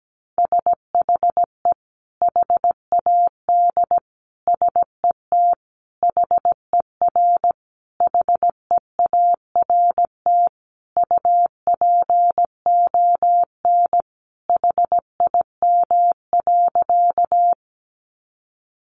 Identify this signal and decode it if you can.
Morse code, heard as a keyed tone, about 17 wpm, 700 Hz. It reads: SHE HAD SET HER HEART UPON HIM.
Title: Sentences from Top 1200 Words 17wpm